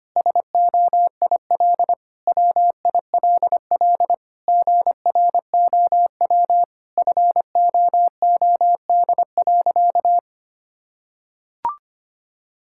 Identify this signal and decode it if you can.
Morse code, heard as a keyed tone, about 25 wpm, 700 Hz. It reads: SOIL WILL GROW FOOD.
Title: Sentences from Top 1000 Words - Review Part 6 - 25wpm